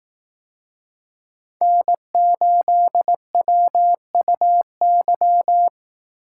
Morse code, heard as a keyed tone, about 18 wpm, 700 Hz. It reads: N8WUY
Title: Speed Racing - USA Call Signs 15wpm